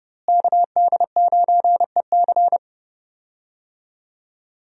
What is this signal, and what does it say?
Morse code, heard as a keyed tone, about 30 wpm, 700 Hz. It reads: KD9EC
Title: USA Call Signs - 5 Characters - Repeated 2x - 30wpm